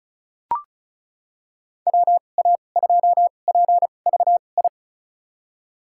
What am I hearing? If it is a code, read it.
Morse code, heard as a keyed tone, about 34 wpm, spaced as for 25 wpm, 700 Hz. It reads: WA2PVI
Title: USA Call Signs - 6 Characters - 2x Character Spacing - 35wpm